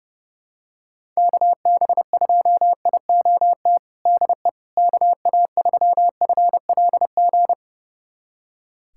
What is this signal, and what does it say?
Morse code, heard as a keyed tone, about 32 wpm, 700 Hz. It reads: KB2IOT DE KA3FLG
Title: USA Callsign DE Callsign - 6 Characters - 30wpm